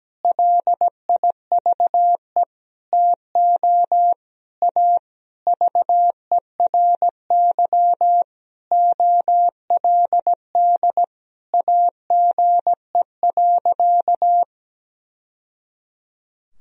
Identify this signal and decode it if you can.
Morse code, heard as a keyed tone, about 17 wpm, 700 Hz. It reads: LIVE TO A VERY OLD AGE.
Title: Sentences from Top 1000 Words - Review Part 3 - 17wpm